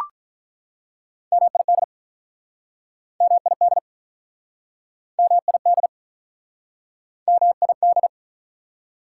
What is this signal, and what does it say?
Morse code, heard as a keyed tone, about 48 wpm, 690 Hz. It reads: MID MID MID MID